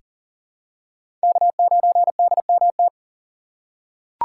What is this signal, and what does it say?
Morse code, heard as a keyed tone, about 40 wpm, 700 Hz. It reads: K9DMT
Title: USA Call Signs - 5 Characters - 4x Character Spacing - 40wpm